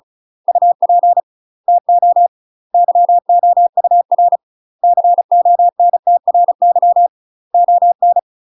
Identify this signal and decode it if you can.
Morse code, heard as a keyed tone, about 36 wpm, 700 Hz. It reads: UP TO YOUR CONTRY ON
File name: UK MASTER QSOs TRAINING / 35WPM / UK MASTER QSOs 0013